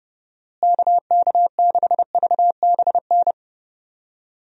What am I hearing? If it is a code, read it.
Morse code, heard as a keyed tone, about 29 wpm, 700 Hz. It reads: KK6VBN
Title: USA Call Signs - 6 Characters - Repeated 3x - 30wpm